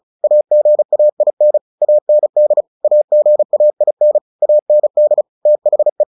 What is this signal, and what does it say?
Morse code, heard as a keyed tone, about 34 wpm, 600 Hz. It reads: AGAIN AND AGAIN AND THE
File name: the old man and the sea 2 35wpm